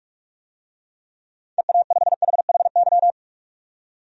Morse code, heard as a keyed tone, about 45 wpm, 700 Hz. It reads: EA5HHY
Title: DXCC-100 Call Signs - 6 Characters - 45wpm